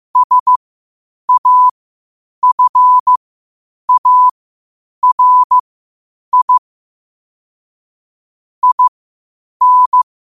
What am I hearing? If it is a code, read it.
Morse code, heard as a keyed tone, about 15 wpm, spaced as for 8 wpm, 1000 Hz. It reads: SAFARI IN
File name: qotdfiles / 8wpm